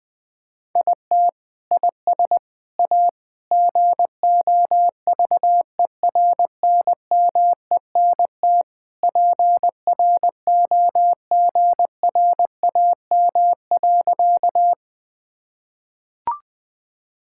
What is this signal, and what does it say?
Morse code, heard as a keyed tone, about 21 wpm, 700 Hz. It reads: IT IS A GOVERNMENT PROGRAM.
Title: Sentences from Top 1000 Words - Review Part 5 - 20wpm